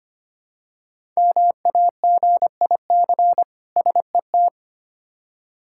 Morse code, heard as a keyed tone, about 25 wpm, 700 Hz. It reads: MAGIC SET
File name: practice / 25wpm / Difficult Words with Hints - 25wpm